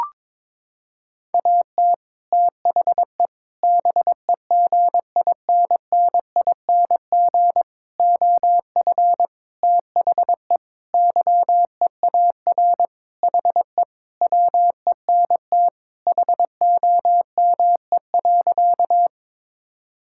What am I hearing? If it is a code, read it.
Morse code, heard as a keyed tone, about 22 wpm, 700 Hz. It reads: AT THE BEGINNING OF THE YEAR HE WENT HOME.
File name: practice / 22wpm / Sentences with 4-Word Phrases - 22wpm